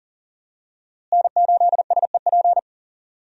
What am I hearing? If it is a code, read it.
Morse code, heard as a keyed tone, about 40 wpm, 700 Hz. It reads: N8SEP
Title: USA Call Signs - 5 Characters - Repeated 3x - 40wpm